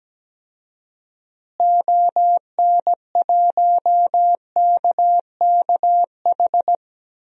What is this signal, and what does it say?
Morse code, heard as a keyed tone, about 17 wpm, 700 Hz. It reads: ON1KKH
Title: DXCC-100 Call Signs - 6 Characters - Repeated 2x - 17wpm